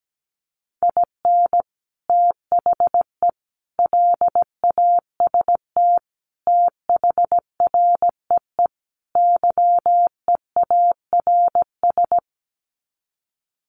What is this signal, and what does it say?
Morse code, heard as a keyed tone, about 17 wpm, 700 Hz. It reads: IN THE LAST THREE YEARS
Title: Sets of 5 Words Encore2 17wpm